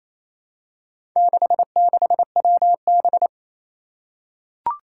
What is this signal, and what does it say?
Morse code, heard as a keyed tone, about 27 wpm, 700 Hz. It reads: 66WB
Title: Four Challenging Letter-Number Combinations 28wpm